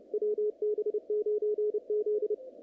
Morse code, heard as a keyed tone, about 28 wpm, 440 Hz, with radio noise. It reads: WB9Z
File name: contests / naqp-cw-k5tr-op-k5pi- -2300